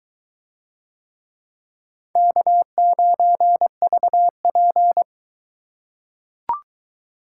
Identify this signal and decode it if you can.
Morse code, heard as a keyed tone, about 23 wpm, 700 Hz. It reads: K9VP